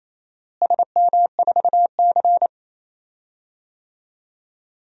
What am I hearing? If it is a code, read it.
Morse code, heard as a keyed tone, about 28 wpm, 700 Hz. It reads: SM4C
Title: DXCC-100 Call Signs - 4 Characters - Repeated 2x - 28wpm